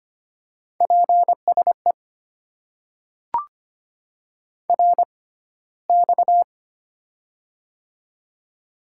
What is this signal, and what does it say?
Morse code, heard as a keyed tone, about 25 wpm, spaced as for 15 wpm, 700 Hz. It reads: PSE RX
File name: Top 90 QSO Elements - 6x Character Spacing - 25wpm